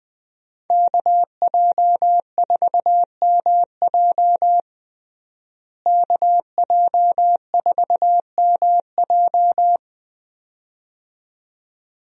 Morse code, heard as a keyed tone, about 20 wpm, 700 Hz. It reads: KJ4MJ KJ4MJ
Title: USA Call Signs - 5 Characters - Repeated 2x - 20wpm